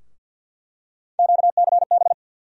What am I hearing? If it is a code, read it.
Morse code, heard as a keyed tone, about 50 wpm, 700 Hz. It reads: QCB
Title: USA Call Sign Postfixes - 3 Letters 50wpm